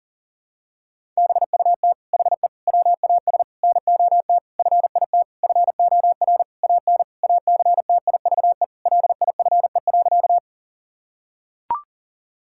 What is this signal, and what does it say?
Morse code, heard as a keyed tone, about 40 wpm, 700 Hz. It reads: BUT HE WAS NOT FIT FOR AN ACTIVE LIFE.
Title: Sentences from Top 1000 Words - Review Part 6 - 40wpm